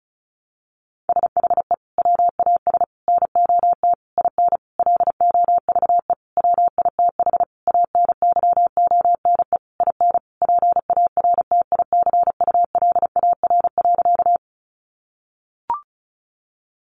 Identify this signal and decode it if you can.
Morse code, heard as a keyed tone, about 34 wpm, 700 Hz. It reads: SHE WAS NOT IN LOVE WITH ANYONE IN PARTICULAR.
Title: Sentences from Top 900 Words 35wpm